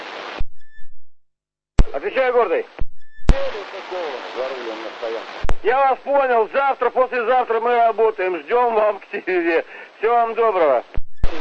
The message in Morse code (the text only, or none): none